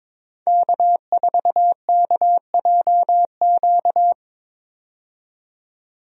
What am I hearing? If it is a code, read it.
Morse code, heard as a keyed tone, about 22 wpm, 700 Hz. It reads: K4KJQ
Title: USA Call Signs - 5 Characters - 22wpm